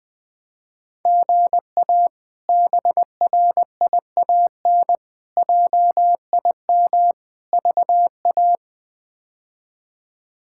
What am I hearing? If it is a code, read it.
Morse code, heard as a keyed tone, about 21 wpm, 700 Hz. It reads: GA BRIAN JIM VA
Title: K1USN SST QSO Practice - 20wpm